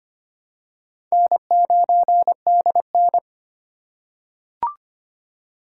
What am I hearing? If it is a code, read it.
Morse code, heard as a keyed tone, about 25 wpm, 700 Hz. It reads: N9DN
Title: USA Call Signs - 4 Characters - 4x Character Spacing - 25wpm